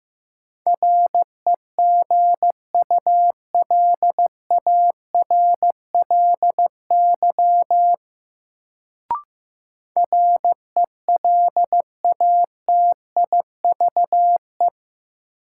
Morse code, heard as a keyed tone, about 16 wpm, 700 Hz. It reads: REGULARLY RELATIVE
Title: Sentences from Top 1400 Words - New Word Drill - 15wpm